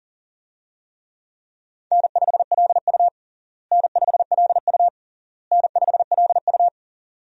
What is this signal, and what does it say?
Morse code, heard as a keyed tone, about 37 wpm, 700 Hz. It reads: N5LU N5LU N5LU
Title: USA Call Signs - 4 Characters - Repeated 3x - 40wpm